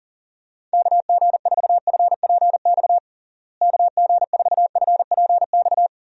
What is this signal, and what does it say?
Morse code, heard as a keyed tone, about 37 wpm, 700 Hz. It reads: KG4FPX KG4FPX
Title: USA Call Signs - 6 Characters - Repeated 3x - 40wpm